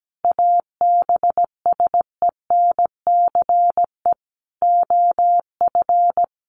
Morse code, heard as a keyed tone, about 17 wpm, 700 Hz. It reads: ABSENCE OF
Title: Sets of 5 Words Encore 17wpm